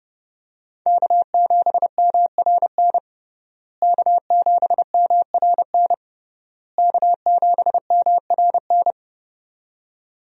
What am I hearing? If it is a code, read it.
Morse code, heard as a keyed tone, about 30 wpm, 700 Hz. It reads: K7MRN K7MRN K7MRN
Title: USA Call Signs - 5 Characters - Repeated 3x - 30wpm